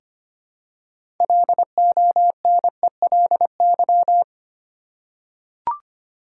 Morse code, heard as a keyed tone, about 25 wpm, 700 Hz. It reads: LONELY
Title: Six Letter Words - 25wpm